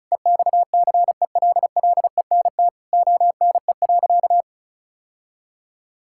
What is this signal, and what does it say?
Morse code, heard as a keyed tone, about 34 wpm, 690 Hz. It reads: EXCELLENT ONE.
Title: Sentences from Top 1000 Words - Review Part 6 - 35wpm